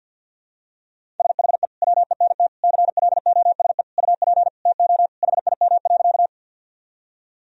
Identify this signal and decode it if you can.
Morse code, heard as a keyed tone, about 52 wpm, 700 Hz. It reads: SHE WENT CLOSE UP TO HIM.
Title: Sentences from Top 1000 Words - Review Part 1 - 50wpm